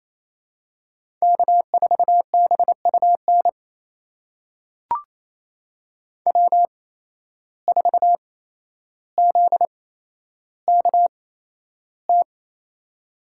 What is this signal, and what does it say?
Morse code, heard as a keyed tone, about 28 wpm, spaced as for 12 wpm, 700 Hz. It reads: K4BUN W4ZKT
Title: USA Call Signs - 5 Characters - 8x Character Spacing - 28wpm